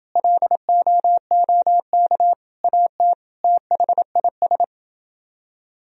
Morse code, heard as a keyed tone, about 30 wpm, 700 Hz. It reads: LOOK AT THIS